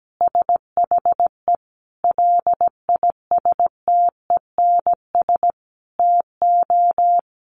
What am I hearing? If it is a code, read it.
Morse code, heard as a keyed tone, about 17 wpm, 700 Hz. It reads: SHE LISTENS TO
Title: Easy Phrases 17wpm